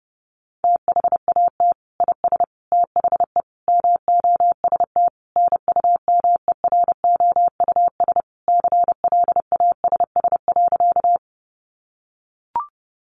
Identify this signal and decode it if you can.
Morse code, heard as a keyed tone, about 30 wpm, 700 Hz. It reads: THAT IS THE MOST NUMEROUS CLASS.
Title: Sentences from Top 1400 Words 30wpm